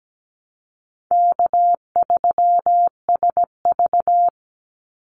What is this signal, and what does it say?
Morse code, heard as a keyed tone, about 17 wpm, 700 Hz. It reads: K3SV
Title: Easy Call Signs - 17wpm